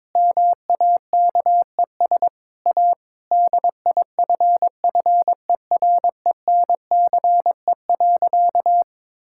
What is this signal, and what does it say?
Morse code, heard as a keyed tone, about 22 wpm, 700 Hz. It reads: MAKES A DIFFERENCE.
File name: Difficult Sentences with Hints - 22wpm